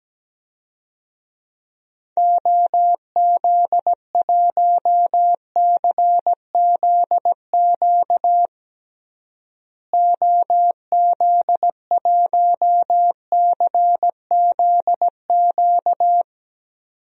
Morse code, heard as a keyed tone, about 17 wpm, 700 Hz. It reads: OZ1CZQ OZ1CZQ
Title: DXCC-100 Call Signs - 6 Characters - Repeated 2x - 17wpm